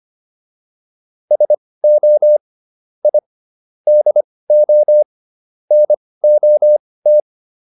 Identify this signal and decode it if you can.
Morse code, heard as a keyed tone, about 25 wpm, spaced as for 18 wpm, 600 Hz. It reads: SO I DO NOT